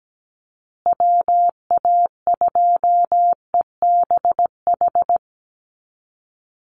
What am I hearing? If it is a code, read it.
Morse code, heard as a keyed tone, about 17 wpm, 700 Hz. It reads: WA2EBH
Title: USA Call Signs Repeated 2x 17wpm